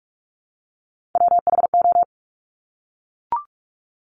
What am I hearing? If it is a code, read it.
Morse code, heard as a keyed tone, about 44 wpm, 700 Hz. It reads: WHO
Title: Top 200 Words Repeated 45wpm